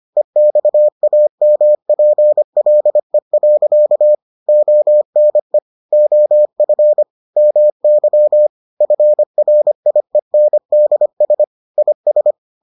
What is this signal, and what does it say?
Morse code, heard as a keyed tone, about 25 wpm, 600 Hz. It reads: EXAMPLE. ONE OF MY FRIENDS IS